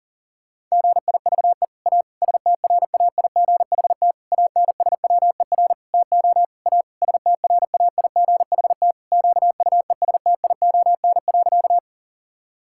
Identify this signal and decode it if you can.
Morse code, heard as a keyed tone, about 40 wpm, 690 Hz. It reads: GIVE A STRAIGHT ANSWER TO A STRAIGHT QUESTION.
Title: Sentences from Top 1000 Words - Review Part 6 - 40wpm